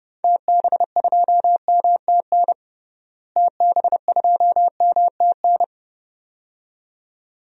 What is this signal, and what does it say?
Morse code, heard as a keyed tone, about 32 wpm, 700 Hz. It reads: TB2MTN TB2MTN
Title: DXCC-100 Call Signs - 6 Characters - Repeated 2x - 30wpm